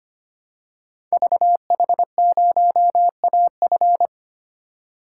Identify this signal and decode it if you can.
Morse code, heard as a keyed tone, about 25 wpm, 700 Hz. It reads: VH0AF
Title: Code Groups - 5 Letter:Numbers - 25wpm